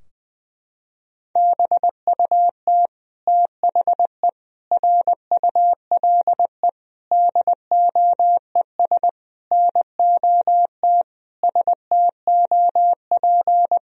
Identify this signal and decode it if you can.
Morse code, heard as a keyed tone, about 20 wpm, 700 Hz. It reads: BUT THE RULE DOES NOT STOP
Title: Sentences from Top 1000 Words - Review Part 2 - 20wpm